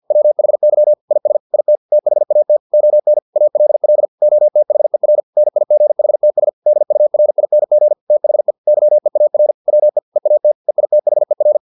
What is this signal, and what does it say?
Morse code, heard as a keyed tone, about 50 wpm, 600 Hz. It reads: WHY IS IT THAT ON ALL OTHER NIGHTS DURING THE YEAR WE EAT EITHER